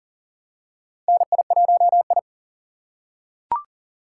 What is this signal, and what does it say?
Morse code, heard as a keyed tone, about 40 wpm, 700 Hz. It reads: NI1I
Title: USA Call Signs - 4 Characters - 4x Character Spacing - 40wpm